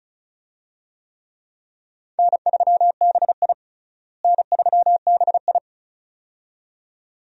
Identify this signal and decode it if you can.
Morse code, heard as a keyed tone, about 36 wpm, 700 Hz. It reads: N3BI N3BI
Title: USA Call Signs - 4 Characters - Repeated 2x - 35wpm